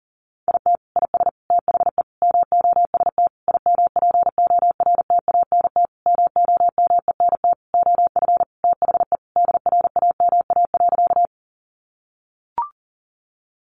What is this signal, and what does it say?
Morse code, heard as a keyed tone, about 40 wpm, 700 Hz. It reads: IT IS THE MOST IMPORTANT MOMENT OF THE DRAMA.
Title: Sentences from Top 1500 Words 40wpm